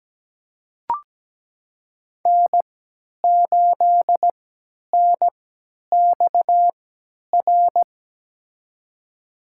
Morse code, heard as a keyed tone, about 17 wpm, spaced as for 10 wpm, 700 Hz. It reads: N8NXR
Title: USA Call Signs - 5 Characters - 3x Character Spacing - 17wpm